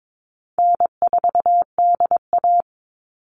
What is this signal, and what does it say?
Morse code, heard as a keyed tone, about 22 wpm, 700 Hz. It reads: N4DA
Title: NA SOTA Chaser Callsigns 22wpm